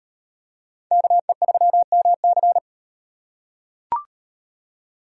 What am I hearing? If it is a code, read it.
Morse code, heard as a keyed tone, about 38 wpm, 700 Hz. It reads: KE3MC